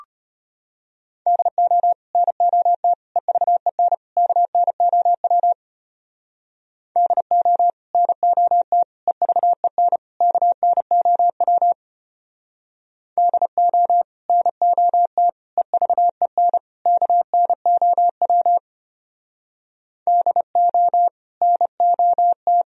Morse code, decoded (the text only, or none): DO NOT EVEN KNOW DO NOT EVEN KNOW DO NOT EVEN KNOW DO NOT